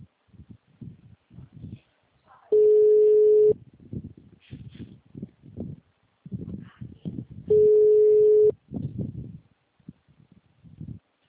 Ringback tone (Europe, 425 Hz).
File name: out- -2022- - - .13927